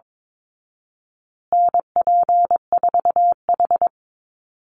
Morse code, heard as a keyed tone, about 22 wpm, 700 Hz. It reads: NP4H
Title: Call Signs Repeated 2x - 22wpm